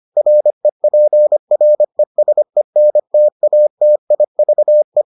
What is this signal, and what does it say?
Morse code, heard as a keyed tone, about 25 wpm, 600 Hz. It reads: REPRESENTATIVE